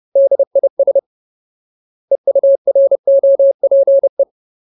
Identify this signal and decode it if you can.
Morse code, heard as a keyed tone, about 30 wpm, 550 Hz. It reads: DIS EUROPE
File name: Ultimate Word List / 30WPM Ultimate